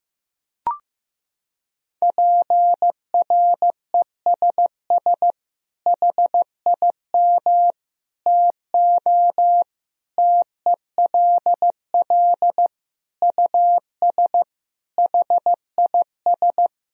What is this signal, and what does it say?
Morse code, heard as a keyed tone, about 15 wpm, 700 Hz. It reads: PRESS HIM TO TELL US HIS
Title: Sentences from Top 1000 Words - Review Part 1 - 15wpm